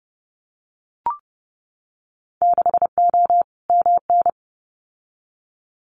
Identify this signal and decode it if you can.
Morse code, heard as a keyed tone, about 30 wpm, 700 Hz. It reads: 6O MN
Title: Winter Field Day Exchanges Encore 30wpm